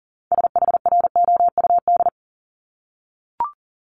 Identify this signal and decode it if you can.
Morse code, heard as a keyed tone, about 42 wpm, 700 Hz. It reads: SHROUD
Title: Multi-letter Phonemes 40wpm